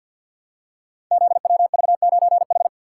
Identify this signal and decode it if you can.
Morse code, heard as a keyed tone, about 47 wpm, 700 Hz. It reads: ZWV9H
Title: Code Groups - 5 Letter:Numbers - 50wpm